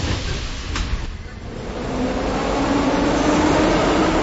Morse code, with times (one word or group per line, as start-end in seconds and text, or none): none